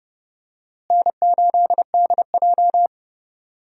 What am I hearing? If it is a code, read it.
Morse code, heard as a keyed tone, about 30 wpm, 700 Hz. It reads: N8DJ
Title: USA Call Signs - 4 Characters - Repeated 2x - 30wpm